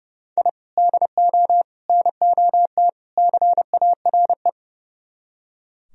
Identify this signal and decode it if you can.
Morse code, heard as a keyed tone, about 40 wpm, 700 Hz. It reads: I DO NOT CARE